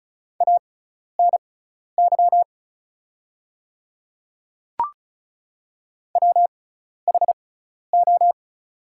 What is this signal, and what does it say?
Morse code, heard as a keyed tone, about 36 wpm, spaced as for 12 wpm, 700 Hz. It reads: ANY WHO
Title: 6x Character Spacing - Top 200 Words - 35wpm